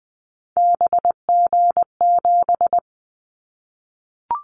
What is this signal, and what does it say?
Morse code, heard as a keyed tone, about 20 wpm, 700 Hz. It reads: BG7
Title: Callsign Prefixes for Top 100 DXCC - 20wpm